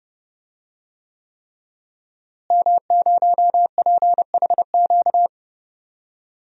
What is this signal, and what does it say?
Morse code, heard as a keyed tone, about 30 wpm, 700 Hz. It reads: M0PHQ
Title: DXCC-100 Call Signs - 5 Characters - Repeated 3x - 30wpm